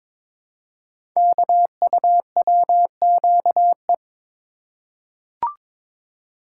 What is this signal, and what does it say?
Morse code, heard as a keyed tone, about 22 wpm, 700 Hz. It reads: KUWQE
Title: Code Groups - 5 Letters - 22wpm